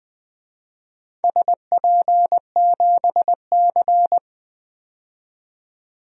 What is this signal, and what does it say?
Morse code, heard as a keyed tone, about 20 wpm, 700 Hz. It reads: SP7C
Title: DXCC-100 Call Signs - 4 Characters - 20wpm